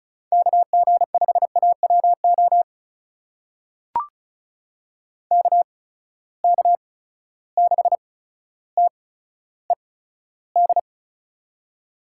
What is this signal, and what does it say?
Morse code, heard as a keyed tone, about 36 wpm, spaced as for 15 wpm, 700 Hz. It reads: KG5AWO KK6TED
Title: USA Call Signs - 6 Characters - 8x Character Spacing - 35wpm